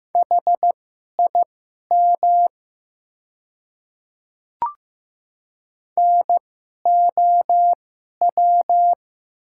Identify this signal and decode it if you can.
Morse code, heard as a keyed tone, about 16 wpm, spaced as for 11 wpm, 700 Hz. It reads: HIM NOW
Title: 2x Character Spacing - Top 500 Words - 15wpm